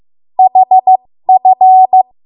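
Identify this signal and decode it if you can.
Morse code, heard as a keyed tone, about 15 wpm, spaced as for 13 wpm, 750 Hz. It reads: HF
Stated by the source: 13 wpm Morse code practice archive transmission.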